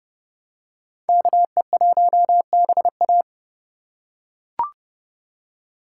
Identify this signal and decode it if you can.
Morse code, heard as a keyed tone, about 29 wpm, 700 Hz. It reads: KE1BA